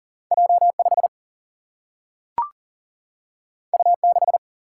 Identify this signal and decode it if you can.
Morse code, heard as a keyed tone, about 39 wpm, 700 Hz. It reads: J5 U6
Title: Single-Letter Single-Number 40wpm